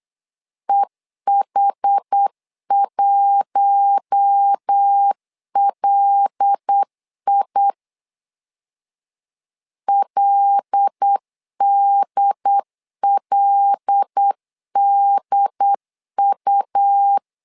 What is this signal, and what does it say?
Morse code, heard as a keyed tone, about 8 wpm, 800 Hz. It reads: EH1LI LDLDU
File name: LEZIONI / 5 LBVC / 5 L 8wpm 2831